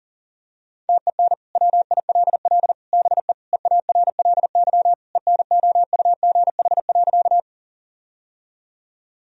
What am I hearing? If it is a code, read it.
Morse code, heard as a keyed tone, about 40 wpm, 700 Hz. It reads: TEN WILL BE EARLY ENOUGH.
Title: Sentences from Top 1000 Words - Review Part 2 - 40wpm